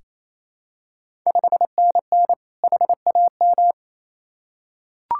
Morse code, heard as a keyed tone, about 28 wpm, 700 Hz. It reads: 5NN HAM
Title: Ontario QSO Party Exchange 28wpm